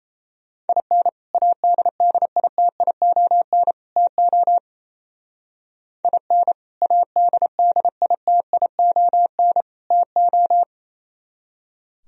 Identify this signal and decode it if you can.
Morse code, heard as a keyed tone, about 32 wpm, 700 Hz. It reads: IN ADDITION TO IN ADDITION TO
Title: Speed-Racing - Sets of 3 Words 28wpm